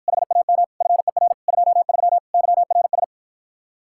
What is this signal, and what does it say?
Morse code, heard as a keyed tone, about 54 wpm, 700 Hz. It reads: HAM FER 23 YRS